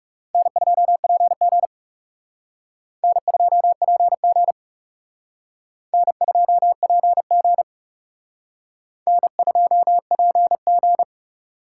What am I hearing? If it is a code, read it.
Morse code, heard as a keyed tone, about 37 wpm, 700 Hz. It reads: N2PG N2PG N2PG N2PG
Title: Speed Racing - USA Call Signs 30wpm